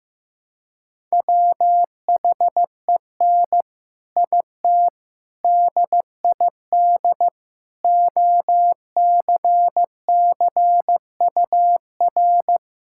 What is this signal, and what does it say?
Morse code, heard as a keyed tone, about 15 wpm, 700 Hz. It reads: WHEN IT DID OCCUR
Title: Sentences from Top 1000 Words - Review Part 6 - 15wpm